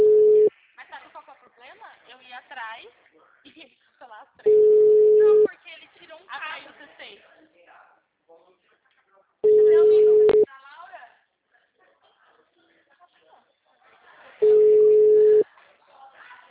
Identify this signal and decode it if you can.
Ringback tone (Europe, 425 Hz), with speech.
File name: out- -2020- - - .65615